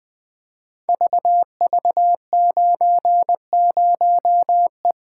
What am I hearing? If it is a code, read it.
Morse code, heard as a keyed tone, about 20 wpm, 700 Hz. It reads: VV90E
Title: Code Groups - 5 Letter:Numbers - 20wpm